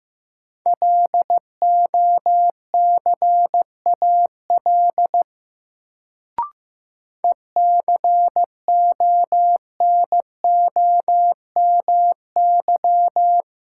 Morse code, heard as a keyed tone, about 15 wpm, 700 Hz. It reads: LOCAL ECONOMY
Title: Sentences from Top 1000 Words - Review Part 3 - New Word Drill - 15wpm